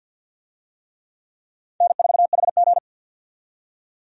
Morse code, heard as a keyed tone, about 52 wpm, 700 Hz. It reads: N4HG
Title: USA Call Signs - 4 Characters - 50wpm